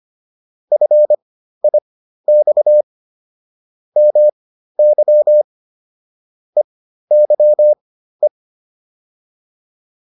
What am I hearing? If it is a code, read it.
Morse code, heard as a keyed tone, about 25 wpm, spaced as for 13 wpm, 600 Hz. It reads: FIX MY EYE